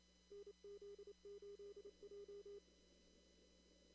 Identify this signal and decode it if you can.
Morse code, heard as a keyed tone, about 27 wpm, 410 Hz. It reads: NZ8J
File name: CW / kt5j wpx cw-op-k5pi